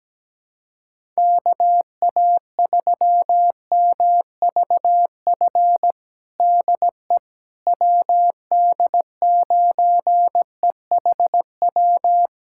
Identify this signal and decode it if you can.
Morse code, heard as a keyed tone, about 17 wpm, 700 Hz. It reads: KA3MVF DE WD9EHW
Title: USA Callsign DE Callsign 17wpm